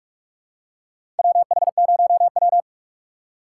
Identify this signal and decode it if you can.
Morse code, heard as a keyed tone, about 43 wpm, 700 Hz. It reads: WH0W
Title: USA Call Signs - 4 Characters - 6x Character Spacing - 45wpm